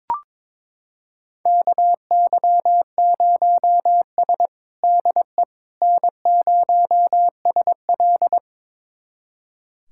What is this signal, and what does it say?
Morse code, heard as a keyed tone, about 22 wpm, 700 Hz. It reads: KY0S DE N0SL
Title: USA Callsign DE Callsign - 4 Characters - 22wpm